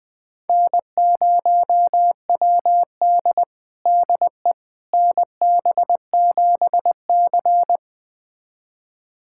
Morse code, heard as a keyed tone, about 20 wpm, 700 Hz. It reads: N0WD DE NB7C
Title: USA Callsign DE Callsign - 4 Characters - 20wpm